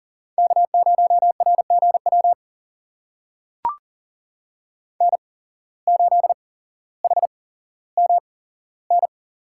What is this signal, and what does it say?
Morse code, heard as a keyed tone, about 38 wpm, spaced as for 17 wpm, 700 Hz. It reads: K0RGW N8HMN
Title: USA Call Signs - 5 Characters - 8x Character Spacing - 40wpm